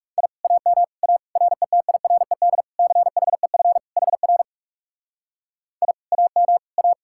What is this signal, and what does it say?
Morse code, heard as a keyed tone, about 39 wpm, 700 Hz. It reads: I AM A RETIRED CHEF HR I AM A